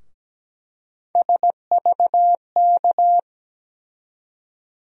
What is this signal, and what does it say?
Morse code, heard as a keyed tone, about 17 wpm, 700 Hz. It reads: SVK
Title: USA Call Sign Postfixes - 3 Letters 17wpm